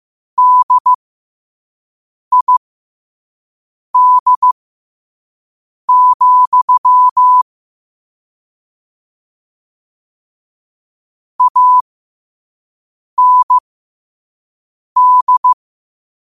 Morse code, heard as a keyed tone, about 15 wpm, spaced as for 5 wpm, 1000 Hz. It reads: DID, AND
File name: qotdfiles / 5wpm